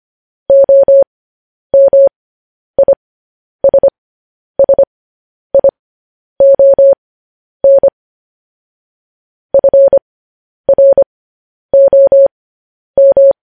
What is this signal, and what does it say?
Morse code, heard as a keyed tone, about 25 wpm, spaced as for 10 wpm, 550 Hz. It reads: OMISSION FROM
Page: News Headlines in Morse Code at 10 WPM